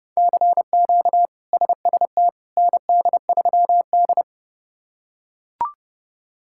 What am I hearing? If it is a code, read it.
Morse code, heard as a keyed tone, about 31 wpm, 700 Hz. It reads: CQ SST ND3D
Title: K1USN SST QSO Practice - 30wpm